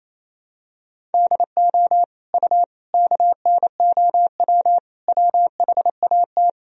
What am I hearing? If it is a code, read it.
Morse code, heard as a keyed tone, about 28 wpm, 700 Hz. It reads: DO U KNOW WHAT
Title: Sentences from Top 300 Words and Abbreviations 28wpm